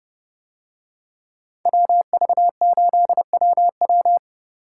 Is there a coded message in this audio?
Yes: Morse code from 1.7 to 4.2 s, heard as a keyed tone, about 28 wpm, 700 Hz. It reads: WV8WW